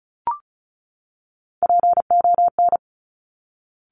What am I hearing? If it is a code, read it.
Morse code, heard as a keyed tone, about 36 wpm, 700 Hz. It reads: PON
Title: Top Three Letter Combinations in Words 35wpm